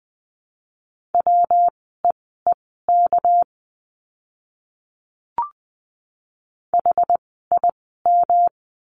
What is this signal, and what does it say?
Morse code, heard as a keyed tone, about 20 wpm, spaced as for 14 wpm, 700 Hz. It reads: WEEK HIM
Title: 244 Lesson 39f - Letter z - Words 20wpm